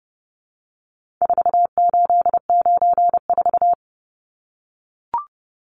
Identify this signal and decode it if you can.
Morse code, heard as a keyed tone, about 29 wpm, 700 Hz. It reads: 4894
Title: Numbers - 4-Digit 30wpm